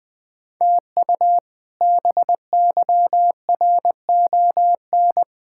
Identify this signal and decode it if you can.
Morse code, heard as a keyed tone, about 20 wpm, 700 Hz. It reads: TU BYRON
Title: K1USN SST QSO Practice - 20wpm